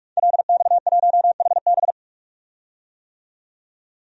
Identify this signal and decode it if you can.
Morse code, heard as a keyed tone, about 46 wpm, 700 Hz. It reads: LX1HB
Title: DXCC-100 Call Signs - 5 Characters - 45wpm